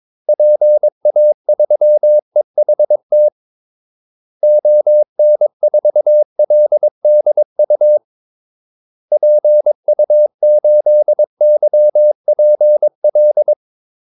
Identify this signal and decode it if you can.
Morse code, heard as a keyed tone, about 22 wpm, 600 Hz. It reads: PA3EHT ON4LDU PU8YPL
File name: Callsigns-22wpm